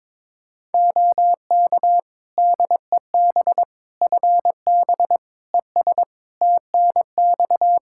Morse code, heard as a keyed tone, about 22 wpm, 700 Hz. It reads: OK DEB FB ES TNX
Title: Practice QSO - Exchange 2 - Info - 22wpm